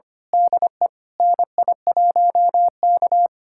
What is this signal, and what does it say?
Morse code, heard as a keyed tone, about 25 wpm, 700 Hz. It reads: DE NI1K
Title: USA Callsign DE Callsign - 4 Characters - 25wpm